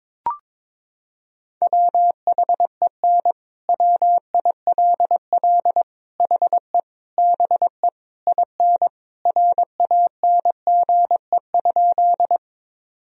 Morse code, heard as a keyed tone, about 21 wpm, 700 Hz. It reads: WHEN WILL HE BE IN RANGE?
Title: Sentences from Top 1000 Words - Review Part 4 - 22wpm